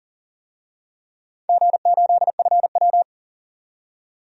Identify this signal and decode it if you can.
Morse code, heard as a keyed tone, about 41 wpm, 700 Hz. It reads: G8FW
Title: DXCC-100 Call Signs - 4 Characters - 40wpm